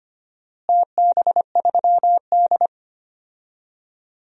Morse code, heard as a keyed tone, about 25 wpm, 700 Hz. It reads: TB3D
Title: DXCC-100 Call Signs - 4 Characters - Repeated 2x - 25wpm